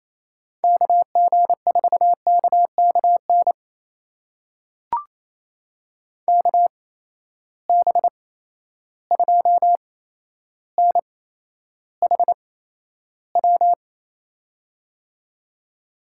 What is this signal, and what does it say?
Morse code, heard as a keyed tone, about 28 wpm, spaced as for 12 wpm, 700 Hz. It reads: KG4KKN KB2NHW
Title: USA Call Signs - 6 Characters - 8x Character Spacing - 28wpm